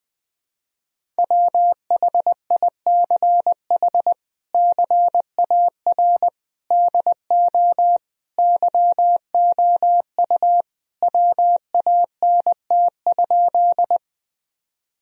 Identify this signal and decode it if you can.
Morse code, heard as a keyed tone, about 21 wpm, 700 Hz. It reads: WHICH CAR DO YOU WANT?
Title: Sentences from Top 1000 Words - Review Part 1 - 20wpm